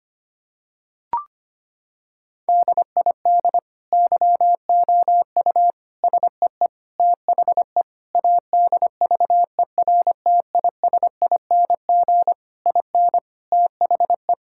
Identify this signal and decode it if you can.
Morse code, heard as a keyed tone, about 25 wpm, 700 Hz. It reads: DID YOU SEE THE ADVERTISING IN THE
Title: Sentences from Top 1000 Words - Review Part 6 - 25wpm